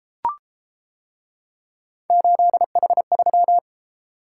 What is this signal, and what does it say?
Morse code, heard as a keyed tone, about 30 wpm, 700 Hz. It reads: 8H3